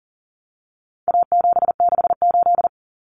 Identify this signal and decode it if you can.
Morse code, heard as a keyed tone, about 40 wpm, 700 Hz. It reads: A768
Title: Common Cut Numbers - 4-Digit 40wpm